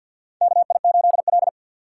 Morse code, heard as a keyed tone, about 49 wpm, 700 Hz. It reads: KI8L
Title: USA Call Signs - 2x Character Spacing - 50wpm